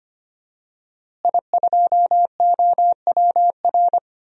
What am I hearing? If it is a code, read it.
Morse code, heard as a keyed tone, about 25 wpm, 700 Hz. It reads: I2OWR
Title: DXCC-100 Call Signs - 5 Characters - Repeated 2x - 25wpm